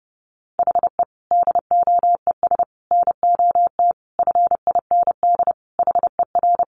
Morse code, heard as a keyed tone, about 30 wpm, 700 Hz. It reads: HE DOES NOT FIND HER
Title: Easy Phrases 30wpm